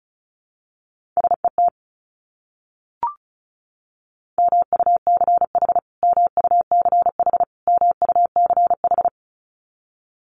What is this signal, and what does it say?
Morse code, heard as a keyed tone, about 34 wpm, 700 Hz. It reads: SET MUCH MUCH MUCH
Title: Top 200 Words Repeated 35wpm